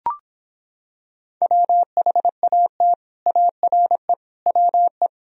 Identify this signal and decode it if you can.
Morse code, heard as a keyed tone, about 26 wpm, 700 Hz. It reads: WHAT ARE WE